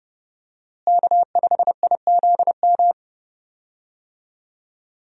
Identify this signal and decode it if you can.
Morse code, heard as a keyed tone, about 30 wpm, 700 Hz. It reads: K5IZM
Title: USA Call Signs - 5 Characters - Repeated 3x - 30wpm